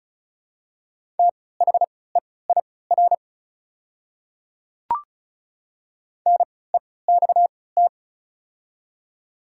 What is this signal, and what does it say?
Morse code, heard as a keyed tone, about 36 wpm, spaced as for 20 wpm, 700 Hz. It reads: THEIR NEXT
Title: 3x Character Spacing - Top 300 Words - 35wpm